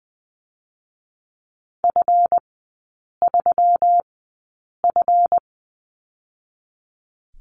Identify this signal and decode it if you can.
Morse code, heard as a keyed tone, about 20 wpm, 690 Hz. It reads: F 3 F
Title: 145 Lesson 24e - Letter y - Three Characters 20wpm